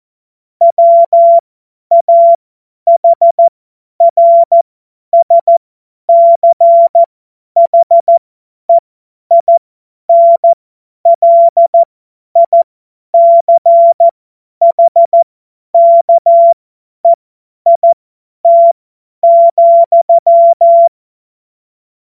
Morse code, heard as a keyed tone, about 14 wpm, spaced as for 10 wpm, 680 Hz. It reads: WAHRSCHEINLICHKEIT,